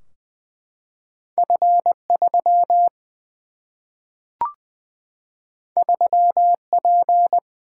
Morse code, heard as a keyed tone, about 20 wpm, 700 Hz. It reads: F 3 3 P
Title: Single-Letter Single-Number 20wpm